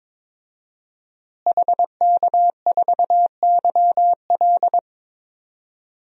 Morse code, heard as a keyed tone, about 22 wpm, 700 Hz. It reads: HK4YL
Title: DXCC-100 Call Signs - 5 Characters - Repeated 3x - 22wpm